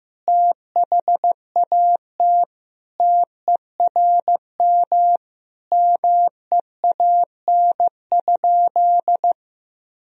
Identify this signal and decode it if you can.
Morse code, heard as a keyed tone, about 15 wpm, 700 Hz. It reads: THAT TERM MEAN?
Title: Sentences from Top 1000 Words - Review Part 4 - 15wpm